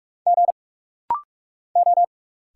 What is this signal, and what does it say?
Morse code, heard as a keyed tone, about 44 wpm, 700 Hz. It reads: G O
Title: Single Letter-Number Alt - Rapid-Fire 45wpm